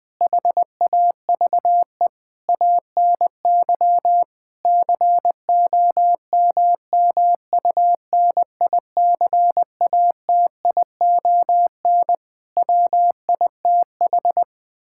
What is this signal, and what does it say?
Morse code, heard as a keyed tone, about 20 wpm, 700 Hz. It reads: HAVE ANY COMMUNICATION WITH
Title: Sentences from Top 1000 Words - Review Part 6 - 20wpm